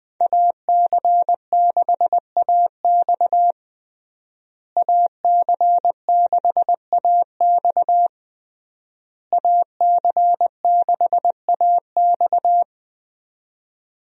Morse code, heard as a keyed tone, about 20 wpm, 700 Hz. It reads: AC6AX AC6AX AC6AX
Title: USA Call Signs - 5 Characters - Repeated 3x - 20wpm